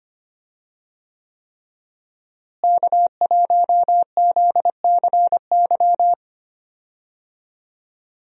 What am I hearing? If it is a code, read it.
Morse code, heard as a keyed tone, about 25 wpm, 700 Hz. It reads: K1ZCY
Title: USA Call Signs - 5 Characters - Repeated 3x - 25wpm